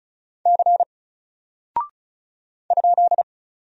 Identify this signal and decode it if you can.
Morse code, heard as a keyed tone, about 36 wpm, 700 Hz. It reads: C ?
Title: Single Character (Letter:Number:Punctuation:Prosign) 35wpm